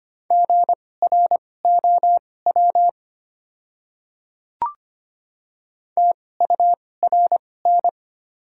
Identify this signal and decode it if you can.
Morse code, heard as a keyed tone, about 25 wpm, spaced as for 18 wpm, 700 Hz. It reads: GROW TURN
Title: 2x Character Spacing - Top 300 Words - 25wpm